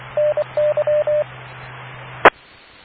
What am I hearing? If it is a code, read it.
Morse code, heard as a keyed tone, about 24 wpm, 600 Hz, with radio noise. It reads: NY E